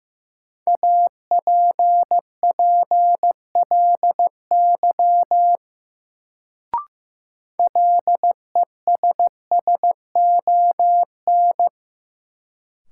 Morse code, heard as a keyed tone, about 15 wpm, 700 Hz. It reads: APPLY LESSON
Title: Sentences from Top 1200 Words - New Word Drill Alt - 15wpm